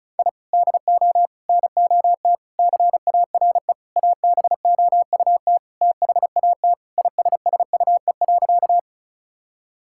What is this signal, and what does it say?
Morse code, heard as a keyed tone, about 34 wpm, 700 Hz. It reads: I DO NOT CARE ABOUT THAT ISSUE.
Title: Sentences from Top 1000 Words - Review Part 6 - 35wpm